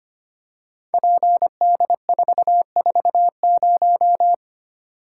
Morse code, heard as a keyed tone, about 25 wpm, 700 Hz. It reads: PD440
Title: Code Groups - 5 Letter:Numbers - 25wpm